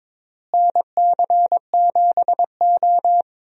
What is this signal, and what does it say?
Morse code, heard as a keyed tone, about 21 wpm, 700 Hz. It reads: NC7O
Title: USA Call Signs - 4 Characters - Repeated 3x - 22wpm